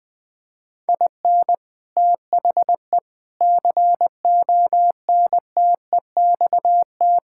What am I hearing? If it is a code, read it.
Morse code, heard as a keyed tone, about 21 wpm, 700 Hz. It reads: IN THE CONTEXT
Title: Sentences with 4-Word Phrases - Spoken Beginning - 20wpm